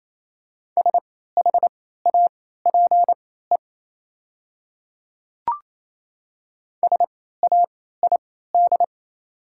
Morse code, heard as a keyed tone, about 28 wpm, spaced as for 16 wpm, 690 Hz. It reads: SHAPE SAID
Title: 3x Character Spacing - Top 500 Words - 28wpm